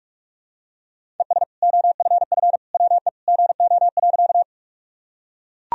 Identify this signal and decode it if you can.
Morse code, heard as a keyed tone, about 41 wpm, 700 Hz. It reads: ES OFF WE GO.
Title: Sentences from Top 200 Words and Abbreviations 45wpm